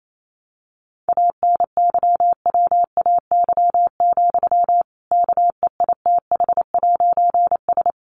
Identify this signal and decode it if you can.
Morse code, heard as a keyed tone, about 28 wpm, 700 Hz. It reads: ANYWAY, KEITH'S
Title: A Sound Of Thunder 28wpm